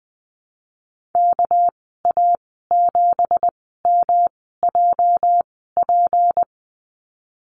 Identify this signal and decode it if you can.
Morse code, heard as a keyed tone, about 20 wpm, 700 Hz. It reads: K A 7 M J P
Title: 238 Lesson 38g - Letter q - Call Signs 20wpm